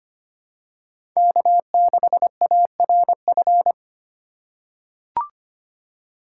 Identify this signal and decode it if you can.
Morse code, heard as a keyed tone, about 25 wpm, 700 Hz. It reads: K6ARF
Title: USA Call Signs - 5 Characters - 3x Character Spacing - 25wpm